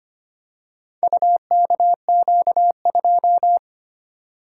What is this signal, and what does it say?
Morse code, heard as a keyed tone, about 25 wpm, 700 Hz. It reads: UKQ2
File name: practice / 25wpm / Four Letter-Number Combinations - 25wpm